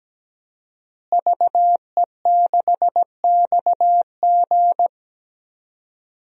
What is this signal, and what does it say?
Morse code, heard as a keyed tone, about 17 wpm, 700 Hz. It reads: VE6XG
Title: DXCC-100 Call Signs - 5 Characters - Repeated 2x - 17wpm